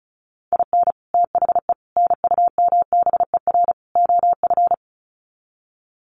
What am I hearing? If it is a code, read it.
Morse code, heard as a keyed tone, about 34 wpm, 700 Hz. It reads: IN THE NUMBER OF
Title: Sets of 4 Words 35wpm